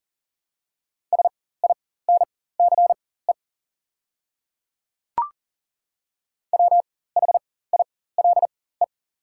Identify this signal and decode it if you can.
Morse code, heard as a keyed tone, about 40 wpm, spaced as for 19 wpm, 690 Hz. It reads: SINCE WHILE
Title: 4x Character Spacing - Top 500 Words - 40wpm